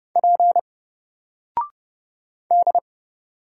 Morse code, heard as a keyed tone, about 31 wpm, 700 Hz. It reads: P D
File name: Single-Letter-Number-30wpm